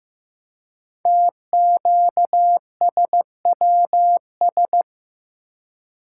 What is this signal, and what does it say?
Morse code, heard as a keyed tone, about 15 wpm, 700 Hz. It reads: TQSWS
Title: Code Groups - 5 Letters - 15wpm